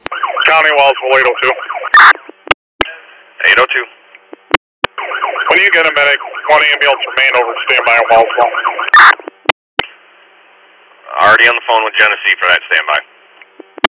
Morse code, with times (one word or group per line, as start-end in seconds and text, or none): none